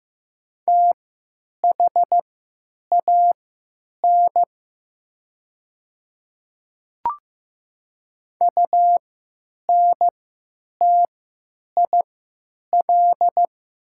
Morse code, heard as a keyed tone, about 15 wpm, spaced as for 9 wpm, 700 Hz. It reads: THAN UNTIL
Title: 3x Character Spacing - Top 300 Words - 15wpm